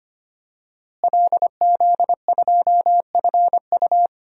Morse code, heard as a keyed tone, about 25 wpm, 700 Hz. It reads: LZ2FU
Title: DXCC-100 Call Signs - 5 Characters - Repeated 2x - 25wpm